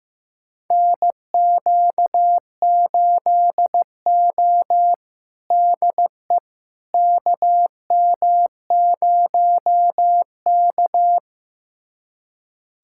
Morse code, heard as a keyed tone, about 16 wpm, 700 Hz. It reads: NQ8O DE KM0K
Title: USA Callsign DE Callsign - 4 Characters - 15wpm